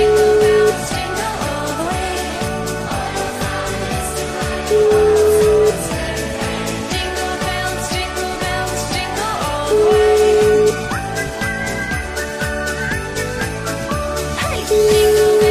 Ringback tone (Europe, 425 Hz), with noise.